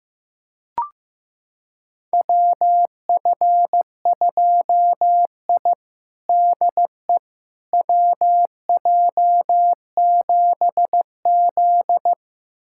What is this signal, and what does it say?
Morse code, heard as a keyed tone, about 16 wpm, 700 Hz. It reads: WF2I DE WJ7Z
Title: USA Callsign DE Callsign - 4 Characters - 15wpm